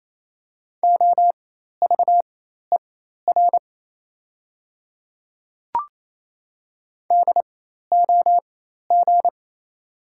Morse code, heard as a keyed tone, about 28 wpm, spaced as for 13 wpm, 700 Hz. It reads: OVER DOG
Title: 4x Character Spacing - Top 500 Words - 28wpm